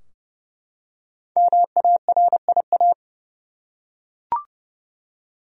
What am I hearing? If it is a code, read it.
Morse code, heard as a keyed tone, about 31 wpm, 700 Hz. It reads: MARIA
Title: USA Common Names - 30wpm